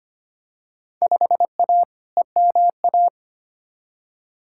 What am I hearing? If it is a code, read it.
Morse code, heard as a keyed tone, about 25 wpm, 700 Hz. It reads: 5A EMA
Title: Field Day Exchanges - 25wpm